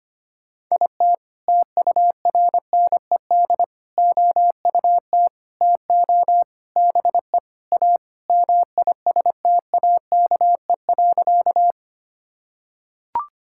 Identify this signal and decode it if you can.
Morse code, heard as a keyed tone, about 25 wpm, 700 Hz. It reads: IT TURNED OUT TO BE A MISTAKE.
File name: practice / 25wpm / Sentences with 4-Word Phrases - 25wpm